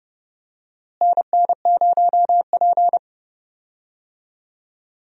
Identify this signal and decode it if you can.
Morse code, heard as a keyed tone, about 29 wpm, 700 Hz. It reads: NN0P
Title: USA Call Signs - 4 Characters - Repeated 3x - 30wpm